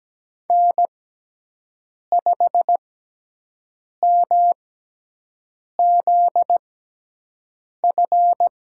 Morse code, heard as a keyed tone, about 17 wpm, spaced as for 6 wpm, 700 Hz. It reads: N5MZF
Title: USA Call Signs - 5 Characters - 6x Character Spacing - 17wpm